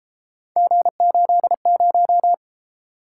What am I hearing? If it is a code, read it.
Morse code, heard as a keyed tone, about 32 wpm, 700 Hz. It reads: G80